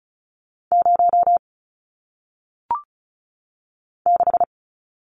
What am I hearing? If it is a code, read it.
Morse code, heard as a keyed tone, about 36 wpm, 700 Hz. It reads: 0 6
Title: Numbers - 1-Digit 35wpm